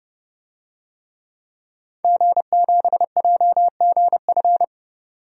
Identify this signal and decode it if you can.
Morse code, heard as a keyed tone, about 30 wpm, 700 Hz. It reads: G7JGF
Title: DXCC-100 Call Signs - 5 Characters - 30wpm